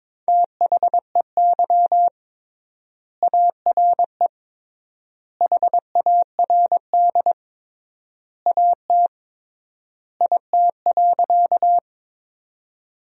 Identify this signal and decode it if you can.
Morse code, heard as a keyed tone, about 22 wpm, 700 Hz. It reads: THEY ARE HARD AT IT.
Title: 3x Word-Spacing - Sentences from Top 300 Words - 22wpm